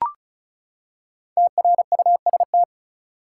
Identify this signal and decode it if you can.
Morse code, heard as a keyed tone, about 36 wpm, 700 Hz. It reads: TRUST
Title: Five Letter Words - 35wpm